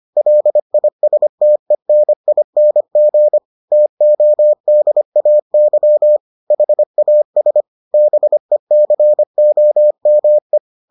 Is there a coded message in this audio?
Yes: Morse code from 0.2 to 10.6 s, heard as a keyed tone, about 25 wpm, 600 Hz. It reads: LISTENING TODAY HAS BECOME